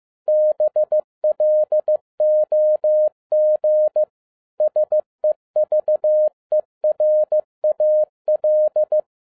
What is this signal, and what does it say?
Morse code, heard as a keyed tone, about 15 wpm, 610 Hz. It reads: BLOG SEVERAL
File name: 2011-07-mp3 / n47 JOHN K3TN 15wpm